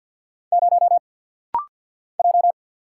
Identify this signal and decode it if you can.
Morse code, heard as a keyed tone, about 49 wpm, 700 Hz. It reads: 0 J
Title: Single Letter-Number Alt - Rapid-Fire 50wpm